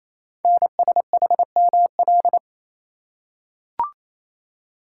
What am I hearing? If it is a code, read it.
Morse code, heard as a keyed tone, about 28 wpm, 700 Hz. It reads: NSHML
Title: Code Groups - 5 Letters - 28wpm